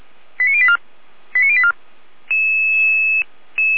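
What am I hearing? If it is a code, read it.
Five-tone selective call (ZVEI), heard as quick runs of five tones, tones of about 70 ms. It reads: 89084 89084
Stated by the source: ZVEI five-tone selective call received on SDR.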